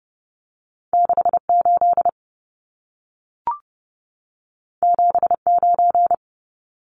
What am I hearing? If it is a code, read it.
Morse code, heard as a keyed tone, about 30 wpm, 700 Hz. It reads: 68 79
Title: Numbers - 2-Digit 30wpm